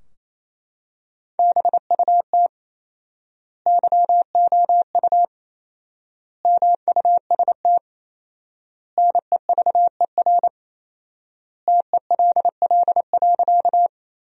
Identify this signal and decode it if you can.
Morse code, heard as a keyed tone, about 28 wpm, 700 Hz. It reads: BUT YOU MUST NEVER TELL.
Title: 4x Word-Spacing - Sentences from Top 300 Words - 28wpm